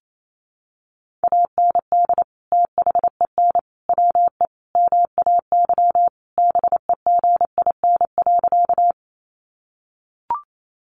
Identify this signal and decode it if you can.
Morse code, heard as a keyed tone, about 28 wpm, 700 Hz. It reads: AND THEN WE MAY BEGIN.
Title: Sentences from Top 300 Words 28wpm